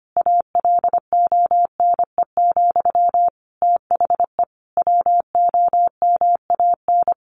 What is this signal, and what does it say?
Morse code, heard as a keyed tone, about 25 wpm, 700 Hz. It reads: ALONE, THE WOMAN
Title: Sentences from Top 1100 Words 25wpm